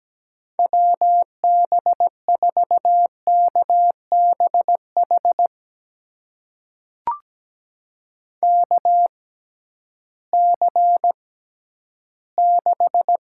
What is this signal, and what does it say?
Morse code, heard as a keyed tone, about 17 wpm, spaced as for 11 wpm, 700 Hz. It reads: WB4KBH KC6
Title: USA Call Signs - 6 Characters - 6x Character Spacing - 17wpm